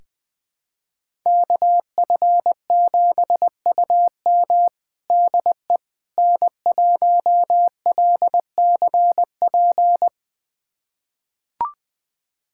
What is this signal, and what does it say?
Morse code, heard as a keyed tone, about 20 wpm, 700 Hz. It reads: KF7UM DE N1LCP
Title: USA Callsign DE Callsign - 5 Characters - 20wpm